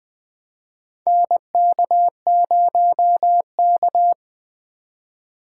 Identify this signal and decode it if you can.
Morse code, heard as a keyed tone, about 20 wpm, 700 Hz. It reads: NK0K
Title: USA Call Signs - 4 Characters - 2x Character Spacing - 20wpm